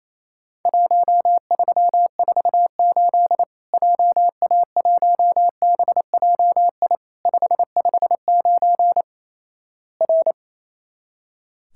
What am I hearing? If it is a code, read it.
Morse code, heard as a keyed tone, about 28 wpm, 700 Hz. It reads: 1348 JA1BJI 559 R
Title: Worked All Europe DX Contest - QTC Exchange - 28wpm